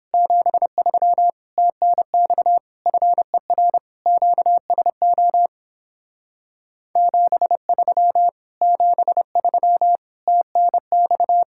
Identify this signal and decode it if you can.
Morse code, heard as a keyed tone, about 29 wpm, 700 Hz. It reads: 73 TNX FER QSO 73 73 TNX